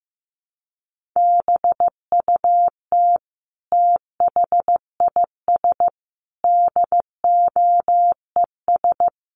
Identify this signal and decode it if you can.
Morse code, heard as a keyed tone, about 15 wpm, 700 Hz. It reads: BUT THIS DOES
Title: Sentences from Top 1400 Words 15wpm